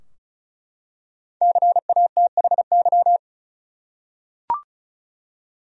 Morse code, heard as a keyed tone, about 34 wpm, 700 Hz. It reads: CATHY
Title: USA Common Names - 35wpm